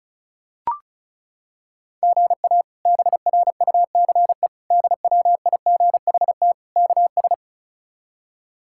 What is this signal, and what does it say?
Morse code, heard as a keyed tone, about 34 wpm, 700 Hz. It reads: GA BRUCE DWIGHT KS
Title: K1USN SST QSO Practice - 35wpm